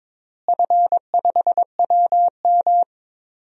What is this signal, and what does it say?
Morse code, heard as a keyed tone, about 22 wpm, 700 Hz. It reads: F5WM
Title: DXCC-100 Call Signs - 4 Characters - Repeated 3x - 22wpm